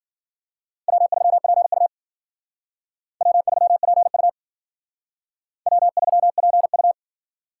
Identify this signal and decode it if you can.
Morse code, heard as a keyed tone, about 51 wpm, 700 Hz. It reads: W3PU W3PU W3PU